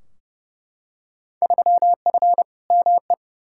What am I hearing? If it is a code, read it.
Morse code, heard as a keyed tone, about 32 wpm, 700 Hz. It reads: 3F ME
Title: Field Day Exchanges - 30wpm